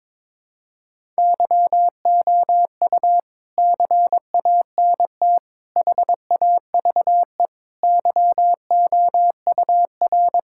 Morse code, heard as a keyed tone, about 21 wpm, 700 Hz. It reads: YOU CANT HAVE YOUR
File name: Sentences with 9-Word Phrases - 22wpm